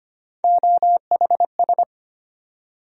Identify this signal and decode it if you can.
Morse code, heard as a keyed tone, about 25 wpm, 700 Hz. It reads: OHS
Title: USA Call Sign Postfixes - 3 Letters 25wpm